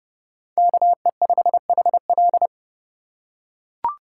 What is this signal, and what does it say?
Morse code, heard as a keyed tone, about 29 wpm, 700 Hz. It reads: KE5HL